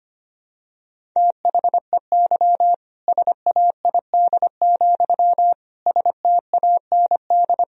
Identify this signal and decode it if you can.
Morse code, heard as a keyed tone, about 25 wpm, 700 Hz. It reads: THEY SAID, STAND
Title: Sentences from Top 1000 Words - Review Part 1 - 25wpm